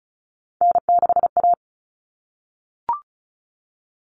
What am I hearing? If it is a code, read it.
Morse code, heard as a keyed tone, about 34 wpm, 700 Hz. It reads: N6A
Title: All Cut Numbers - 3-Digit 35wpm